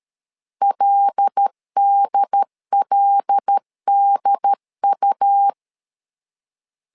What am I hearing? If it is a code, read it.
Morse code, heard as a keyed tone, about 13 wpm, 800 Hz. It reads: LDLDU
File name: LEZIONI / 5 LBVC / 5 L 12wpm 1913